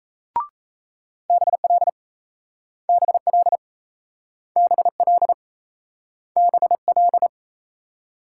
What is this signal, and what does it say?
Morse code, heard as a keyed tone, about 39 wpm, 700 Hz. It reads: BL BL BL BL